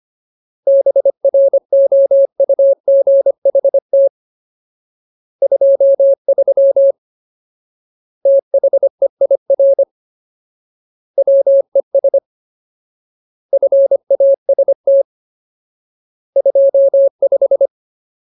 Morse code, heard as a keyed tone, about 25 wpm, 550 Hz. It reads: BROUGHT 23 THEIR WES FAST 25
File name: Ultimate Word List / 25WPM Ultimate